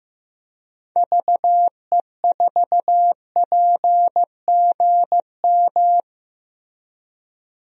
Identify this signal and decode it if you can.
Morse code, heard as a keyed tone, about 16 wpm, 700 Hz. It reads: VE4PGM
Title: DXCC-100 Call Signs - 6 Characters - Repeated 2x - 15wpm